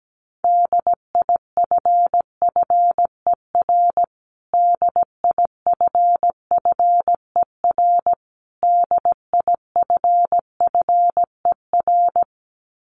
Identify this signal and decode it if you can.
Morse code, heard as a keyed tone, about 17 wpm, 700 Hz. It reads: DIFFER DIFFER DIFFER
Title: Top 500 Words Repeated 17wpm